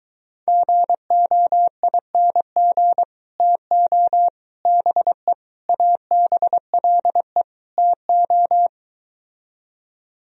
Morse code, heard as a keyed tone, about 22 wpm, 700 Hz. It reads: GOING TO BE ABLE TO